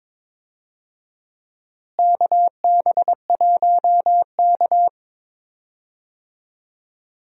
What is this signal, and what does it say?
Morse code, heard as a keyed tone, about 22 wpm, 700 Hz. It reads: KB1K
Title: USA Call Signs - 4 Characters - Repeated 2x - 22wpm